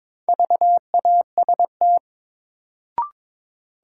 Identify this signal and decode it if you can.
Morse code, heard as a keyed tone, about 22 wpm, 700 Hz. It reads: VAST
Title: Sets of 4 Words - New Word Drill - 22wpm